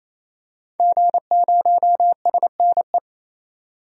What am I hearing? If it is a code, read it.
Morse code, heard as a keyed tone, about 28 wpm, 700 Hz. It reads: G0SNE
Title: DXCC-100 Call Signs - 5 Characters - 28wpm